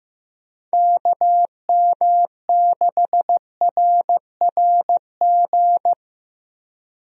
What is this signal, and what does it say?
Morse code, heard as a keyed tone, about 16 wpm, 700 Hz. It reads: KM6RRG
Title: USA Call Signs - 6 Characters - Repeated 2x - 15wpm